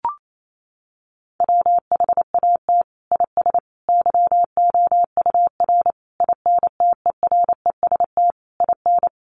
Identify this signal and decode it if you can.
Morse code, heard as a keyed tone, about 28 wpm, 700 Hz. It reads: WHAT IS YOUR INTEREST IN
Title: Sentences from Top 1300 Words 28wpm